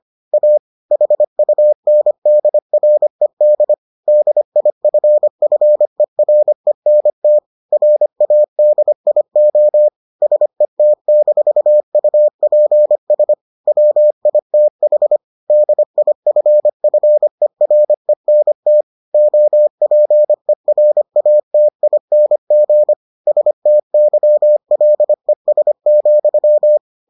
Morse code, heard as a keyed tone, about 25 wpm, 600 Hz. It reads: A HUNDRED DIFFERENT RADIO SET-UPS WITH DIFFERENT OPERATING STYLES,